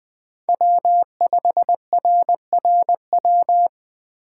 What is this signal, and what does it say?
Morse code, heard as a keyed tone, about 24 wpm, 700 Hz. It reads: W5RRW
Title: Speed Racing - USA Call Signs 20wpm